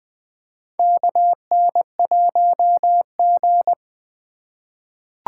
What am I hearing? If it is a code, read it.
Morse code, heard as a keyed tone, about 20 wpm, 700 Hz. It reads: KN1G
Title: USA Call Signs - 4x Character Spacing - 20wpm